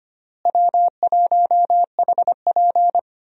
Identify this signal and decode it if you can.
Morse code, heard as a keyed tone, about 25 wpm, 700 Hz. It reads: W1HP
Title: USA Call Signs - 4 Characters - Repeated 3x - 25wpm